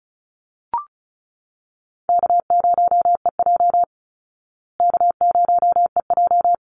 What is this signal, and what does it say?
Morse code, heard as a keyed tone, about 34 wpm, 700 Hz. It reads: K0EJ K0EJ
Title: Easy Call Signs Repeated 2x - 35wpm